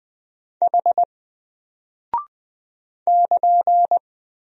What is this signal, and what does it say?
Morse code, heard as a keyed tone, about 20 wpm, 700 Hz. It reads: H <KN>
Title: Single Character (Letter:Number:Punctuation:Prosign) Alt 20wpm